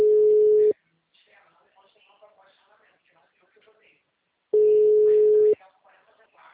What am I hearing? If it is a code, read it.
Ringback tone (Europe, 425 Hz), with speech.